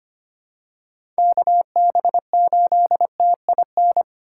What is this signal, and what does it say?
Morse code, heard as a keyed tone, about 25 wpm, 700 Hz. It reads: KB8TIN
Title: USA Call Signs - 6 Characters - 2x Character Spacing - 25wpm